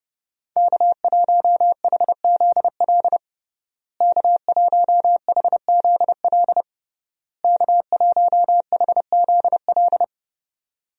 Morse code, heard as a keyed tone, about 30 wpm, 700 Hz. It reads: K1HZL K1HZL K1HZL
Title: USA Call Signs - 5 Characters - Repeated 3x - 30wpm